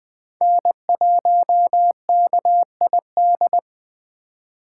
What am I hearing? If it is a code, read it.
Morse code, heard as a keyed tone, about 20 wpm, 700 Hz. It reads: N1KID
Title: USA Call Signs - 5 Characters - 8x Character Spacing - 20wpm